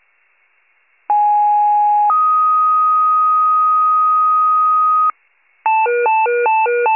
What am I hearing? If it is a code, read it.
Two-tone page, heard as a one-second tone, then a three-second tone, about 830 then 1250 Hz. It is heard from 1.1 to 5.1 s.
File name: pager / 74 FIRE / wcalert